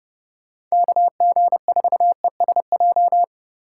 Morse code, heard as a keyed tone, about 36 wpm, 700 Hz. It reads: KG4ESJ
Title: Speed Racing - USA Call Signs 30wpm